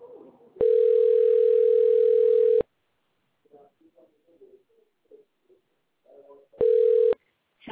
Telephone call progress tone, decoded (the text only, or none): ringback tone